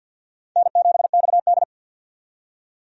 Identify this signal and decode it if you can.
Morse code, heard as a keyed tone, about 50 wpm, 700 Hz. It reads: N7XD
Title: USA Call Signs - 4 Characters - 50wpm